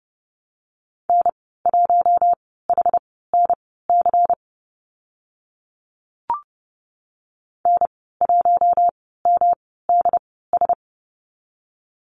Morse code, heard as a keyed tone, about 31 wpm, spaced as for 17 wpm, 700 Hz. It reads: N1HNC N1MDS
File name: Lesson 19g - Letter w - Call Signs 30wpm